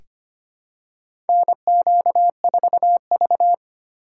Morse code, heard as a keyed tone, about 25 wpm, 700 Hz. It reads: NQ4V
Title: USA Call Signs - 4x Character Spacing - 25wpm